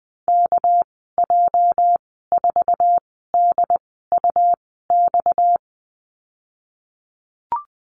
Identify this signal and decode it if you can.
Morse code, heard as a keyed tone, about 20 wpm, spaced as for 14 wpm, 700 Hz. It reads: KJ4DUX
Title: 245 Lesson 39g - Letter z - Call Signs 20wpm